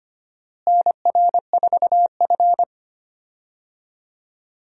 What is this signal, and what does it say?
Morse code, heard as a keyed tone, about 25 wpm, 700 Hz. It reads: NR4F
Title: USA Call Signs - 4 Characters - Repeated 2x - 25wpm